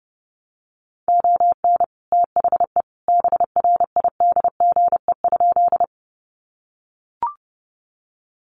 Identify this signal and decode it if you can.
Morse code, heard as a keyed tone, about 30 wpm, 700 Hz. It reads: ON THE BRIDGE?
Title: Sentences from Top 1200 Words 30wpm